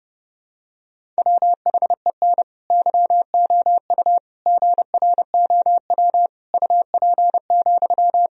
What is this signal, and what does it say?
Morse code, heard as a keyed tone, about 32 wpm, 700 Hz. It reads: WHEN YOU GROW UP,
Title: Sentences from Top 1000 Words - Review Part 1 - 30wpm